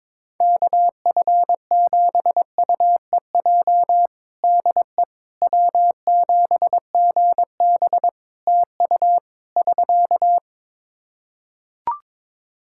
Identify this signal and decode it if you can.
Morse code, heard as a keyed tone, about 22 wpm, 700 Hz. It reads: KF7UEJ DE W7GB TU <SK>
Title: Practice QSO - Exchange 4 - Ending - 22wpm